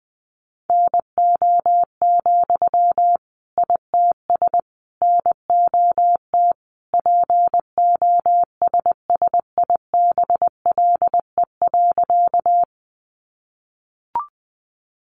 Morse code, heard as a keyed tone, about 20 wpm, 700 Hz. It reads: NO, ITS NOT POSSIBLE.
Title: Sentences from Top 600 Words 20wpm